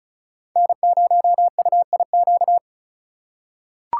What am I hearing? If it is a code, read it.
Morse code, heard as a keyed tone, about 34 wpm, 700 Hz. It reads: N0UIQ
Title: USA Call Signs - 5 Characters - 2x Character Spacing - 35wpm